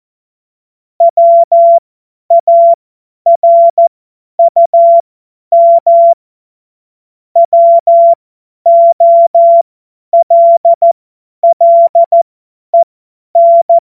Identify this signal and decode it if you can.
Morse code, heard as a keyed tone, about 14 wpm, spaced as for 10 wpm, 680 Hz. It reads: WARUM WOLLEN
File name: Morsekurs / Witze / Witz 06 Februar 2024 50 F70